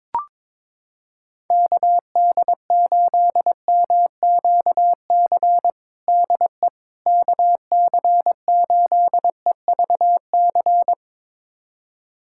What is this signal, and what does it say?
Morse code, heard as a keyed tone, about 22 wpm, 700 Hz. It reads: KD8MQC DE KC8EVC
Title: USA Callsign DE Callsign 22wpm